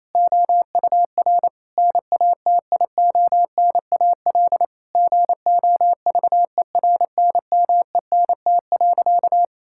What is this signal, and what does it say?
Morse code, heard as a keyed tone, about 28 wpm, 700 Hz. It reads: OUR NATIONAL GOVERNMENT.
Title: Sentences from Top 1000 Words - Review Part 5 - 28wpm